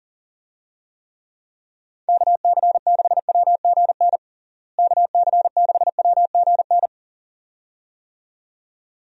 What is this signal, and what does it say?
Morse code, heard as a keyed tone, about 40 wpm, 700 Hz. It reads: KC6WGN KC6WGN
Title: USA Call Signs - 6 Characters - Repeated 2x - 40wpm